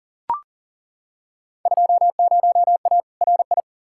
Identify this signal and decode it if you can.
Morse code, heard as a keyed tone, about 37 wpm, 700 Hz. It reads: 20A RI
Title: Field Day Exchanges Encore - 40wpm